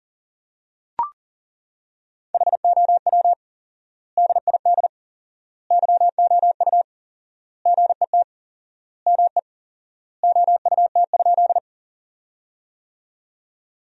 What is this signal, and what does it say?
Morse code, heard as a keyed tone, about 37 wpm, 700 Hz. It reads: HOW DID YOU GET ME OUT?
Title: 4x Word-Spacing - Sentences from Top 200 Words - 40wpm